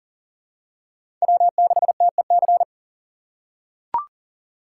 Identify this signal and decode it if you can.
Morse code, heard as a keyed tone, about 40 wpm, 700 Hz. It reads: W6TEC
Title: USA Call Signs - 5 Characters - 4x Character Spacing - 40wpm